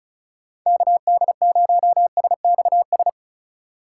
Morse code, heard as a keyed tone, about 36 wpm, 700 Hz. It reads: KD0SXS
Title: USA Call Signs - 6 Characters - 3x Character Spacing - 35wpm